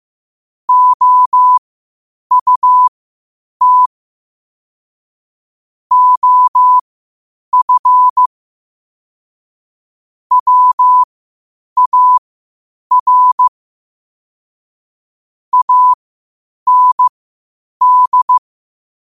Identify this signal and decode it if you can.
Morse code, heard as a keyed tone, about 15 wpm, spaced as for 8 wpm, 1000 Hz. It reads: OUT OF WAR AND
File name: qotdfiles / 8wpm